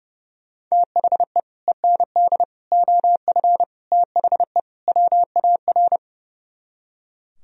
Morse code, heard as a keyed tone, about 37 wpm, 700 Hz. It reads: THE END OF THE WAR
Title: Speed-Racing - Sets of 5 Words 30wpm